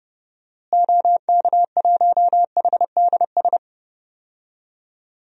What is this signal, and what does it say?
Morse code, heard as a keyed tone, about 32 wpm, 700 Hz. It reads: OK1HDS
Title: DXCC-100 Call Signs - 6 Characters - Repeated 3x - 30wpm